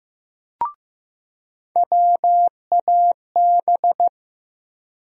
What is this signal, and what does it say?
Morse code, heard as a keyed tone, about 15 wpm, 700 Hz. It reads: WAB
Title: USA Call Sign Postfixes - 3 Letters 15wpm